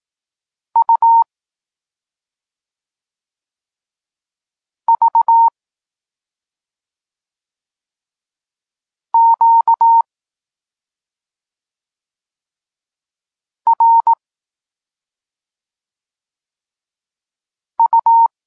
Morse code, heard as a keyed tone, about 18 wpm, 920 Hz. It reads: U V Q R U
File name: Tekst-34 4